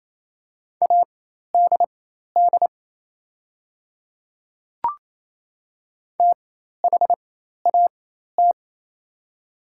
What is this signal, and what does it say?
Morse code, heard as a keyed tone, about 28 wpm, spaced as for 13 wpm, 690 Hz. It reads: ADD THAT
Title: 4x Character Spacing - Top 200 Words - 28wpm